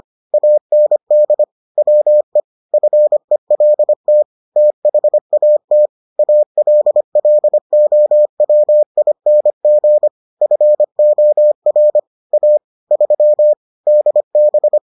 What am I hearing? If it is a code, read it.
Morse code, heard as a keyed tone, about 25 wpm, 600 Hz. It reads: AND WE FELT THAT ALLOWING FOR A 3 DB